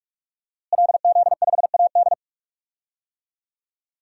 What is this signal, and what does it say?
Morse code, heard as a keyed tone, about 46 wpm, 700 Hz. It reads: LZ5AD
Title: DXCC-100 Call Signs - 5 Characters - Repeated 2x - 45wpm